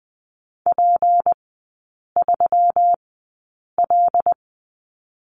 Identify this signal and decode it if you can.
Morse code, heard as a keyed tone, about 19 wpm, spaced as for 8 wpm, 690 Hz. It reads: P3L
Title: 159 Lesson 26e - Letter g - Three Characters 20wpm